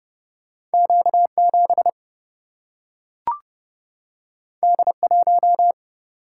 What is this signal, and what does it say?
Morse code, heard as a keyed tone, about 30 wpm, 700 Hz. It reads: Q 7 D 1
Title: Single-Letter Single-Number 30wpm